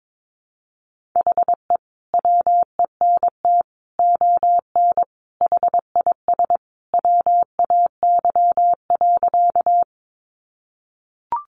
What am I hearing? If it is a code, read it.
Morse code, heard as a keyed tone, about 22 wpm, 700 Hz. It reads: HE WENT ON HIS WAY.
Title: Sentences from Top 200 Words 22wpm